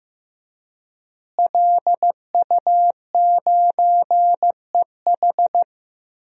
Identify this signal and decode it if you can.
Morse code, heard as a keyed tone, about 15 wpm, 700 Hz. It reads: LU9EH
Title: DXCC-100 Call Signs - 5 Characters - Repeated 3x - 15wpm